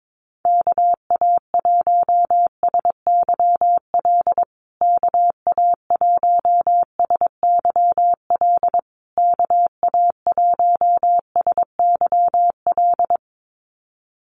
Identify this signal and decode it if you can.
Morse code, heard as a keyed tone, about 22 wpm, 700 Hz. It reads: KA1SYL KA1SYL KA1SYL
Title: USA Call Signs Repeated 3x 22wpm